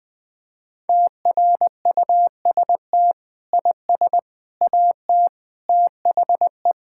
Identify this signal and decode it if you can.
Morse code, heard as a keyed tone, about 21 wpm, 700 Hz. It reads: TRUST IS AT THE
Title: Sentences with 5-Word Phrases - 20wpm